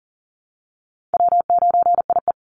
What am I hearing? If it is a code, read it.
Morse code, heard as a keyed tone, about 40 wpm, 700 Hz. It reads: W9IE
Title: Call Signs Repeated 2x - 40wpm